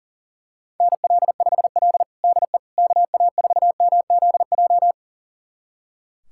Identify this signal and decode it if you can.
Morse code, heard as a keyed tone, about 40 wpm, 700 Hz. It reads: NL5L DE KA4MZJ
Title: USA Callsign DE Callsign 40wpm